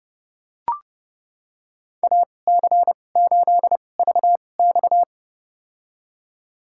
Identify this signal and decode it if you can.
Morse code, heard as a keyed tone, about 30 wpm, spaced as for 22 wpm, 700 Hz. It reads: AC8VX
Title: USA Call Signs - 5 Characters - 2x Character Spacing - 30wpm